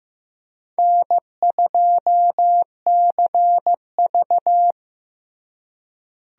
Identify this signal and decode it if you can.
Morse code, heard as a keyed tone, about 15 wpm, 700 Hz. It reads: N2CV
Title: USA Call Signs - 4 Characters - Repeated 2x - 15wpm